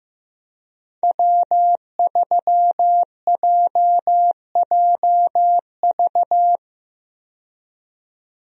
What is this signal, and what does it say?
Morse code, heard as a keyed tone, about 15 wpm, 700 Hz. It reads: W3JJV
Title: USA Call Signs - 5 Characters - 15wpm